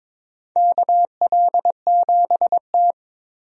Morse code, heard as a keyed tone, about 22 wpm, 700 Hz. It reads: KL7T
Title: DXCC-100 Call Signs - 4 Characters - Repeated 2x - 22wpm